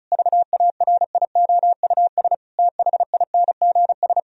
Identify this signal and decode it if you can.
Morse code, heard as a keyed tone, about 36 wpm, 700 Hz. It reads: VARIOUS THINGS
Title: Sentences from Top 1000 Words - Review Part 4 - 35wpm